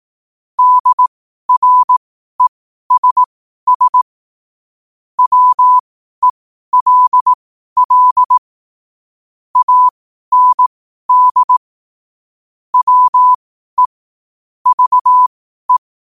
Morse code, heard as a keyed tone, about 18 wpm, spaced as for 12 wpm, 1000 Hz. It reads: DRESS WELL AND WE VE
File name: qotdfiles / 12wpm